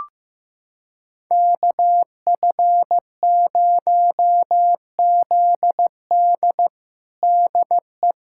Morse code, heard as a keyed tone, about 15 wpm, 700 Hz. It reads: KF0ZD DE
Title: USA Callsign DE Callsign - 5 Characters - 15wpm